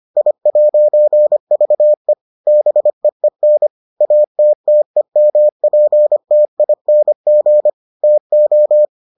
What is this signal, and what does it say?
Morse code, heard as a keyed tone, about 25 wpm, 600 Hz. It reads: I'VE BEEN ATTEMPTING TO